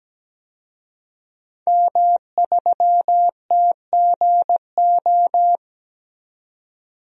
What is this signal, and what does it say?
Morse code, heard as a keyed tone, about 17 wpm, 700 Hz. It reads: M3TGO
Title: DXCC-100 Call Signs - 5 Characters - 17wpm